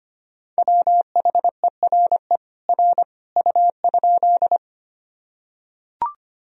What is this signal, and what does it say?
Morse code, heard as a keyed tone, about 25 wpm, 700 Hz. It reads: WHERE R U?
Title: Sentences from Top 200 Words and Abbreviations 25wpm